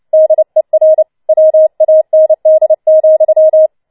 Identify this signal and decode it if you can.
Morse code, heard as a keyed tone, about 29 wpm, 630 Hz. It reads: DER WAND,
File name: Schneewittchen Teil3 30wpm